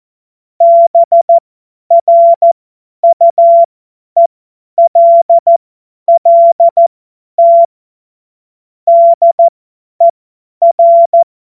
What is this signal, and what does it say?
Morse code, heard as a keyed tone, about 14 wpm, spaced as for 10 wpm, 680 Hz. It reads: BRUELLT DER